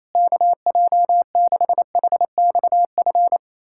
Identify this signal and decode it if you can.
Morse code, heard as a keyed tone, about 28 wpm, 700 Hz. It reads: KJ6HXF
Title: USA Callsign DE Callsign 28wpm